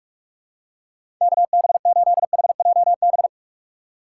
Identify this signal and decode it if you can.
Morse code, heard as a keyed tone, about 42 wpm, 700 Hz. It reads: KB8HJB
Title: USA Call Signs - 6 Characters - 6x Character Spacing - 45wpm